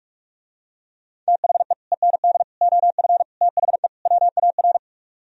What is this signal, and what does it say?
Morse code, heard as a keyed tone, about 48 wpm, 700 Hz. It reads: THE END OF THE WAR